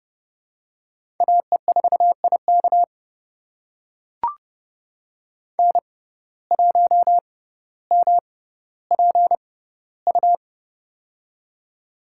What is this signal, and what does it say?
Morse code, heard as a keyed tone, about 30 wpm, spaced as for 15 wpm, 700 Hz. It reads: AE4IK N1MPU
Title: USA Call Signs - 5 Characters - 6x Character Spacing - 30wpm